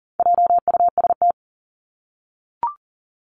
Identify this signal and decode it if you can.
Morse code, heard as a keyed tone, about 40 wpm, 700 Hz. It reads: JUST
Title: Top 300 Words Repeated 40wpm